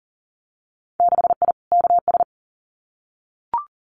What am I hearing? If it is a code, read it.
Morse code, heard as a keyed tone, about 40 wpm, 700 Hz. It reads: 6I KS
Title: Winter Field Day Exchanges 40wpm